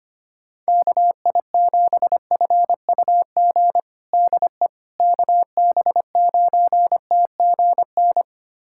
Morse code, heard as a keyed tone, about 25 wpm, 700 Hz. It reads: KI7FUG DE KB9TGN
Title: USA Callsign DE Callsign - 6 Characters - 25wpm